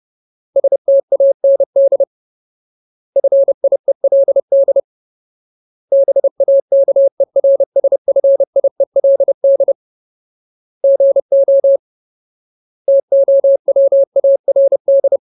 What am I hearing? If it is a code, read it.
Morse code, heard as a keyed tone, about 30 wpm, 550 Hz. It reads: STAND FIELD BAKERSFIELD GO TOWARD
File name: Ultimate Word List / 30WPM Ultimate